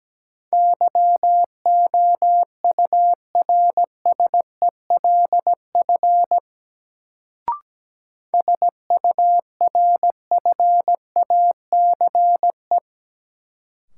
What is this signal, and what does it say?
Morse code, heard as a keyed tone, about 17 wpm, 700 Hz. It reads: YOURSELF SURFACE
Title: Sentences from Top 600 Words - New Word Drill - 17wpm